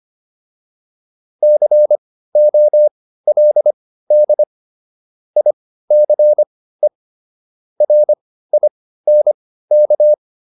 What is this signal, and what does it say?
Morse code, heard as a keyed tone, about 25 wpm, spaced as for 15 wpm, 600 Hz. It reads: COLD ICE RINK